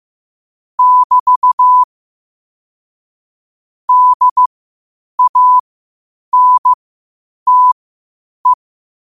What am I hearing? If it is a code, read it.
Morse code, heard as a keyed tone, about 15 wpm, spaced as for 8 wpm, 1000 Hz. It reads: <BT> DANTE
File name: qotdfiles / 8wpm